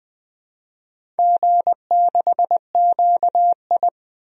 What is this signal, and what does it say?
Morse code, heard as a keyed tone, about 20 wpm, 700 Hz. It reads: G6QI
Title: DXCC-100 Call Signs - 4 Characters - 20wpm